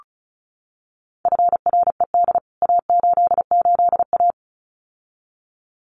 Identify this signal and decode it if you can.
Morse code, heard as a keyed tone, about 34 wpm, 700 Hz. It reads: FRED A88A
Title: CWT Exchange 35wpm